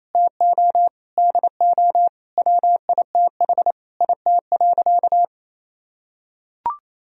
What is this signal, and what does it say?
Morse code, heard as a keyed tone, about 28 wpm, 700 Hz. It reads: TO DO WITH IT.
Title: Sentences from Top 1000 Words - Review Part 2 - 28wpm